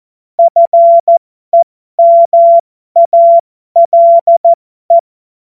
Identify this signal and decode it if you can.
Morse code, heard as a keyed tone, about 14 wpm, spaced as for 12 wpm, 680 Hz. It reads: FEMALE
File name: Morsekurs / Witze